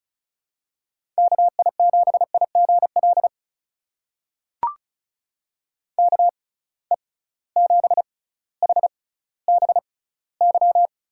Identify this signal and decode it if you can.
Morse code, heard as a keyed tone, about 36 wpm, spaced as for 18 wpm, 700 Hz. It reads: KI7IGL KE7HBY
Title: USA Call Signs - 6 Characters - 6x Character Spacing - 35wpm